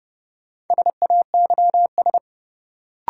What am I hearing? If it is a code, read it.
Morse code, heard as a keyed tone, about 30 wpm, 700 Hz. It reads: SAYS
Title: Sets of 2 Words - New Word Drill - 30wpm